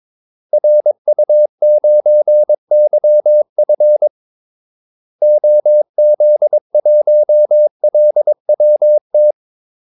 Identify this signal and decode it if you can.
Morse code, heard as a keyed tone, about 22 wpm, 600 Hz. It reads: RU9YF OZ1LWT
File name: Callsigns-22wpm